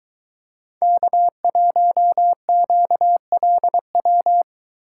Morse code, heard as a keyed tone, about 22 wpm, 700 Hz. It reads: K1QLW